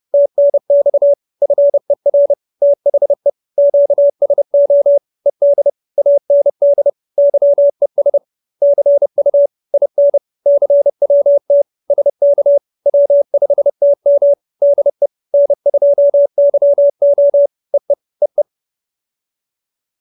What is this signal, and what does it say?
Morse code, heard as a keyed tone, about 30 wpm, 570 Hz. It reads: TNX FER THE QSO ED AND YES CU IN CWT SK W5TM DE N2YO EE EE